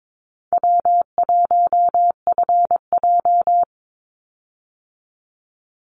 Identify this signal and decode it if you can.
Morse code, heard as a keyed tone, about 22 wpm, 700 Hz. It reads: W1FJ
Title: Easy Call Signs Repeated 2x - 22wpm